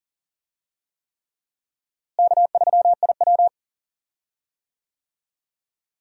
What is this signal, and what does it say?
Morse code, heard as a keyed tone, about 40 wpm, 700 Hz. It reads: K3IW
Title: USA Call Signs - 4 Characters - Repeated 3x - 40wpm